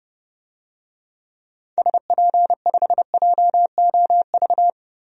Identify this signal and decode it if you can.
Morse code, heard as a keyed tone, about 30 wpm, 700 Hz. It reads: SP5JOV
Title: DXCC-100 Call Signs - 6 Characters - 30wpm